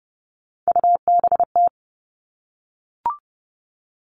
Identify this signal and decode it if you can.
Morse code, heard as a keyed tone, about 30 wpm, 700 Hz. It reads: UBT
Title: All Cut Numbers - 3-Digit 30wpm